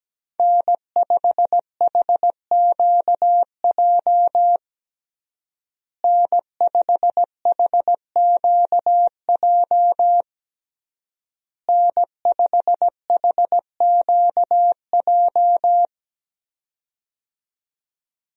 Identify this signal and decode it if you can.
Morse code, heard as a keyed tone, about 17 wpm, 700 Hz. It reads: N5HQJ N5HQJ N5HQJ
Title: USA Call Signs - 5 Characters - Repeated 3x - 17wpm